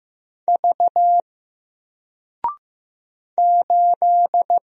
Morse code, heard as a keyed tone, about 15 wpm, 700 Hz. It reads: V 8
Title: Single Character (Letter:Number:Punctuation:Prosign) 15wpm